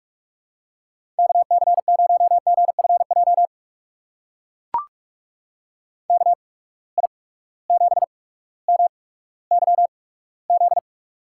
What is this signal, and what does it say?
Morse code, heard as a keyed tone, about 44 wpm, spaced as for 19 wpm, 700 Hz. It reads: KC0GFJ KI7MYZ
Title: USA Call Signs - 6 Characters - 8x Character Spacing - 45wpm